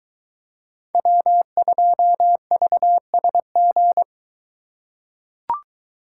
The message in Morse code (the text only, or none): W2VSG